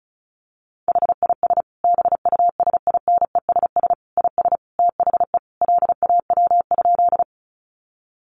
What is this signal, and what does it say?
Morse code, heard as a keyed tone, about 34 wpm, 700 Hz. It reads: HIS BUSINESS IS THE LAW?
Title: Sentences from Top 600 Words 35wpm